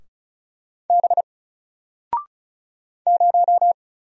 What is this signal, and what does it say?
Morse code, heard as a keyed tone, about 34 wpm, 700 Hz. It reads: B 0
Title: Single Letter-Number 35wpm